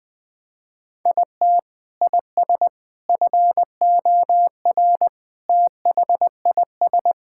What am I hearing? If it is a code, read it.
Morse code, heard as a keyed tone, about 20 wpm, 700 Hz. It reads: IT IS FOR THIS
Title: Sentences with 6-Word Phrases - 20wpm